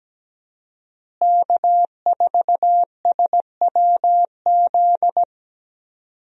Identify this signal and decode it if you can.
Morse code, heard as a keyed tone, about 17 wpm, 700 Hz. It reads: K4SWZ
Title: USA Call Signs - 5 Characters - Repeated 2x - 17wpm